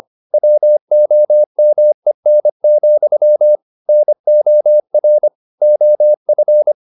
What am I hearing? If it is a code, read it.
Morse code, heard as a keyed tone, about 25 wpm, 600 Hz. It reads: WOMEN, NOR OF
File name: the old man and the sea part 1 25wpm